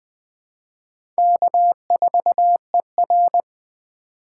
Morse code, heard as a keyed tone, about 20 wpm, 700 Hz. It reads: K4ER
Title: USA Call Signs - 4 Characters - 2x Character Spacing - 20wpm